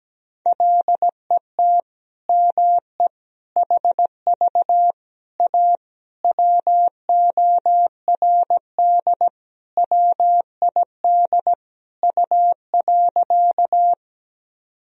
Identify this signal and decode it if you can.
Morse code, heard as a keyed tone, about 17 wpm, 700 Hz. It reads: LET ME HV A WORD WID U.
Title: Sentences from Top 300 Words and Abbreviations 17wpm